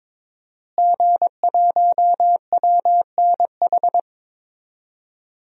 Morse code, heard as a keyed tone, about 21 wpm, 700 Hz. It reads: G1WNH
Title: DXCC-100 Call Signs - 5 Characters - Repeated 3x - 22wpm